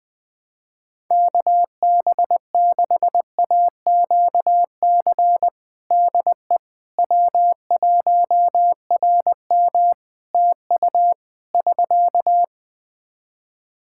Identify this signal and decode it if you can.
Morse code, heard as a keyed tone, about 20 wpm, 700 Hz. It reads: KB6AQC DE W1RM TU <SK>
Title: Practice QSO - Exchange 4 - Ending - 20wpm